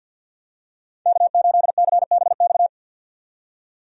Morse code, heard as a keyed tone, about 52 wpm, 700 Hz. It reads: K8CBX
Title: USA Call Signs - 5 Characters - 50wpm